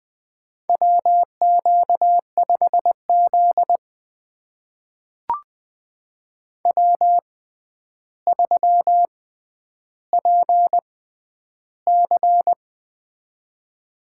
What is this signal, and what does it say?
Morse code, heard as a keyed tone, about 20 wpm, spaced as for 10 wpm, 700 Hz. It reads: WQ5Z W3PC
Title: USA Call Signs - 4 Characters - 6x Character Spacing - 20wpm